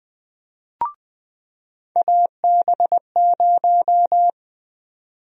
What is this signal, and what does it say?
Morse code, heard as a keyed tone, about 21 wpm, 700 Hz. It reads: AB0
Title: USA Call Sign Prefixes - 20wpm